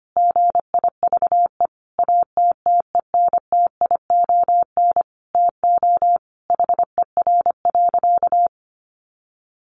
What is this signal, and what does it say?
Morse code, heard as a keyed tone, about 25 wpm, 700 Hz. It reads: GIVE ATTENTION TO HER.
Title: Sentences from Top 1000 Words 25wpm